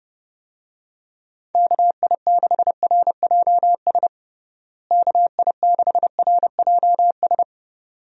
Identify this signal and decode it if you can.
Morse code, heard as a keyed tone, about 30 wpm, 700 Hz. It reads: KI6RJS KI6RJS
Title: USA Call Signs - 6 Characters - Repeated 2x - 30wpm